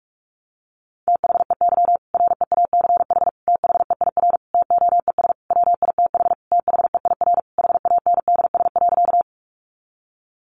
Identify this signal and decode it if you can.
Morse code, heard as a keyed tone, about 44 wpm, 700 Hz. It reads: THEY REACH THEIR TOES WITH THEIR HANDS.
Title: Easy Phrases 45wpm